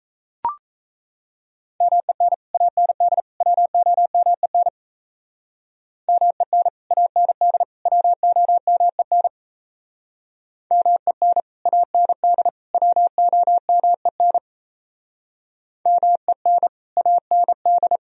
Morse code, heard as a keyed tone, about 34 wpm, 700 Hz. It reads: MEN AND WOMEN MEN AND WOMEN MEN AND WOMEN MEN AND
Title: Speed-Racing - Sets of 3 Words 28wpm